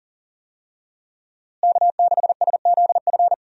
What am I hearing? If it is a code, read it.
Morse code, heard as a keyed tone, about 40 wpm, 700 Hz. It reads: K6SZF
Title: USA Call Signs - 5 Characters - Repeated 3x - 40wpm